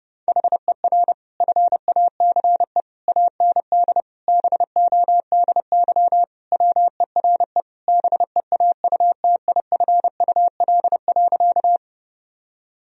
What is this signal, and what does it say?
Morse code, heard as a keyed tone, about 32 wpm, 700 Hz. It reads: HER FACE AND BODY WERE BEAUTIFUL.
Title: Sentences from Top 1000 Words - Review Part 2 - 30wpm